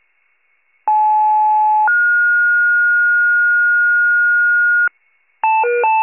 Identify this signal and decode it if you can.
Two-tone page, heard as a one-second tone, then a three-second tone, about 860 then 1430 Hz, with noise. It is heard from 0.9 to 4.9 s.